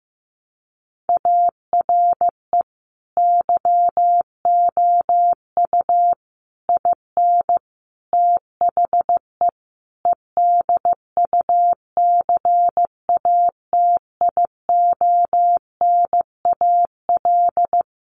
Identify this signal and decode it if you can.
Morse code, heard as a keyed tone, about 15 wpm, 700 Hz. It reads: ARE YOU IN THE EDUCATIONAL
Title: Sentences from Top 1200 Words 15wpm